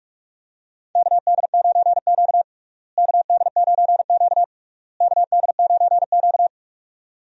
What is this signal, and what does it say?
Morse code, heard as a keyed tone, about 46 wpm, 700 Hz. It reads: KD9Q KD9Q KD9Q
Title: USA Call Signs - 4 Characters - Repeated 3x - 45wpm